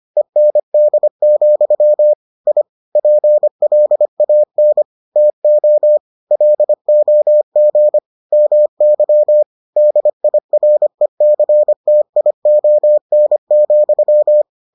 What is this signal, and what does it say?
Morse code, heard as a keyed tone, about 25 wpm, 600 Hz. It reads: END, I PLAN TO LOG MY DIRECTION,